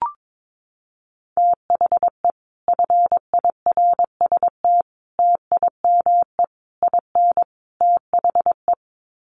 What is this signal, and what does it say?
Morse code, heard as a keyed tone, about 22 wpm, 700 Hz. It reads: THE FIRST TIME IN THE
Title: Sets of 5 Words Encore 22wpm